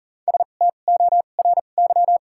Morse code, heard as a keyed tone, about 40 wpm, 700 Hz. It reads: S T O R Y
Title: 2x Character Spacing - Top 300 Words - 40wpm